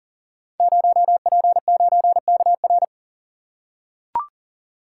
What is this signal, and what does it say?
Morse code, heard as a keyed tone, about 42 wpm, 700 Hz. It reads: 0P9KR
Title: Code Groups - 5 Letter:Numbers - 40wpm